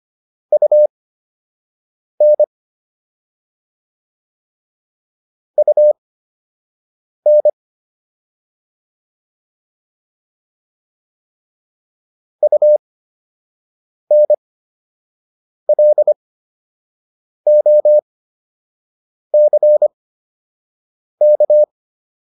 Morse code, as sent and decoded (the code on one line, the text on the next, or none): ..- -. / ..- -. / ..- -. .-.. --- -.-. -.-
UN UN UNLOCK